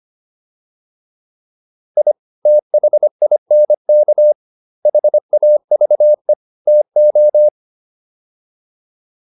Morse code, heard as a keyed tone, about 25 wpm, 600 Hz. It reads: I THINK HAVE TO